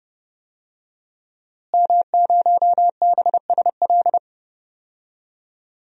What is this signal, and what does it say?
Morse code, heard as a keyed tone, about 32 wpm, 700 Hz. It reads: M0BSL
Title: DXCC-100 Call Signs - 5 Characters - Repeated 2x - 30wpm